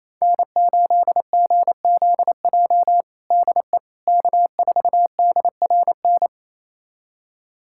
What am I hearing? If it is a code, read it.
Morse code, heard as a keyed tone, about 27 wpm, 700 Hz. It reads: N8GZJ DE K4DRN
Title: USA Callsign DE Callsign 28wpm